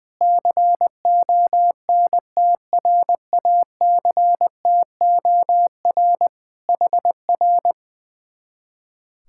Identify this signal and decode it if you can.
Morse code, heard as a keyed tone, about 21 wpm, 700 Hz. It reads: CONTRACTOR HR
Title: Speed-Racing Common QSO Phrases - 20wpm